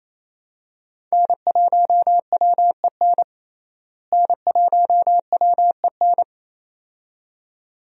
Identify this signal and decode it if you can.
Morse code, heard as a keyed tone, about 27 wpm, 700 Hz. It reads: N1WEN N1WEN
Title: USA Call Signs - 5 Characters - Repeated 2x - 28wpm